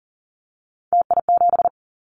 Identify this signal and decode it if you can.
Morse code, heard as a keyed tone, about 41 wpm, 700 Hz. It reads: TI7
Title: Callsign Prefixes for Top 100 DXCC - 40wpm